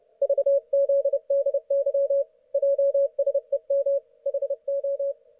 Morse code, heard as a keyed tone, about 31 wpm, 560 Hz. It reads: VZDY JSEM HO
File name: Archiv / CUC179 / ok1fhi cul179